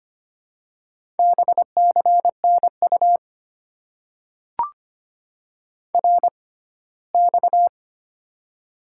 Morse code, heard as a keyed tone, about 25 wpm, spaced as for 17 wpm, 700 Hz. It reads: BCNU RX
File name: Top 90 QSO Elements - 6x Character Spacing - 25wpm